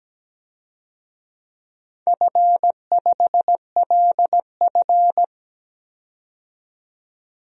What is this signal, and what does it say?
Morse code, heard as a keyed tone, about 17 wpm, 700 Hz. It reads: F5LF
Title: DXCC-100 Call Signs - 4 Characters - 17wpm